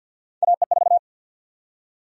Morse code, heard as a keyed tone, about 51 wpm, 700 Hz. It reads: AE4
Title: USA Call Sign Prefixes - 50wpm